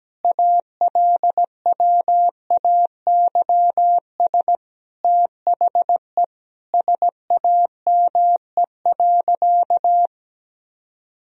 Morse code, heard as a keyed tone, about 17 wpm, 700 Hz. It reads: ALWAYS THE SAME.
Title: Sentences from Top 300 Words and Abbreviations 17wpm